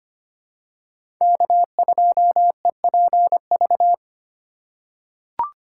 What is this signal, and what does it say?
Morse code, heard as a keyed tone, about 25 wpm, 700 Hz. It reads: K2EPV
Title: USA Call Signs - 5 Characters - 3x Character Spacing - 25wpm